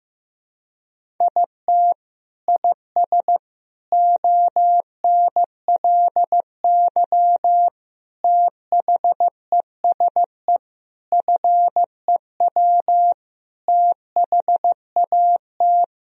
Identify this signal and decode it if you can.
Morse code, heard as a keyed tone, about 16 wpm, 700 Hz. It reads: IT IS ONLY THESE FEW THAT
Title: Sentences from Top 1000 Words - Review Part 1 - 15wpm